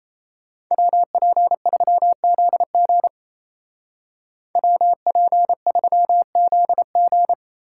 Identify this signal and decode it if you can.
Morse code, heard as a keyed tone, about 33 wpm, 700 Hz. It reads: WP3ZG WP3ZG
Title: Speed Racing - USA Call Signs 28wpm